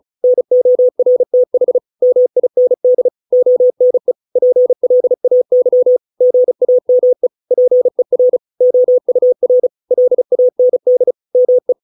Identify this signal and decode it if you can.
Morse code, heard as a keyed tone, about 35 wpm, 500 Hz. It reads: NORTH MIND ONE PLAY GAME PER OUR LAND ME